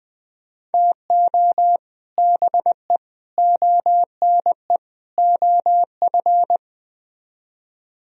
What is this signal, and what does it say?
Morse code, heard as a keyed tone, about 21 wpm, 700 Hz. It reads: TO BE ONE OF